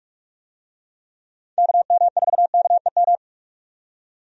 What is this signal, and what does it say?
Morse code, heard as a keyed tone, about 45 wpm, 700 Hz. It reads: KM4KEM
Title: USA Call Signs - 6 Characters - 45wpm